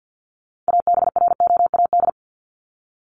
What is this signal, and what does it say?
Morse code, heard as a keyed tone, about 50 wpm, 700 Hz. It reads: ABROAD
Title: Top 1751-2000 Words 50wpm